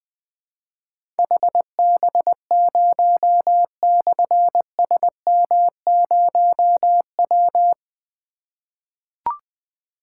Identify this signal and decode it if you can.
Morse code, heard as a keyed tone, about 20 wpm, 700 Hz. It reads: HB0/SM0W
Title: Difficult Call Signs 20wpm